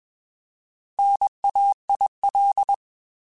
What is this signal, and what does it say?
Morse code, heard as a keyed tone, about 21 wpm, 800 Hz. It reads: NAIL
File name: Alphabetical / Alphabetical mp3s / Ns @20